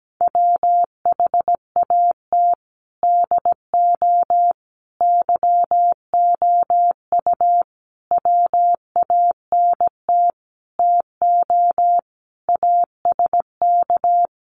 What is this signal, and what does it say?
Morse code, heard as a keyed tone, about 17 wpm, 700 Hz. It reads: WHAT DO YOU WANT TO ASK
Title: Sentences from Top 200 Words 17wpm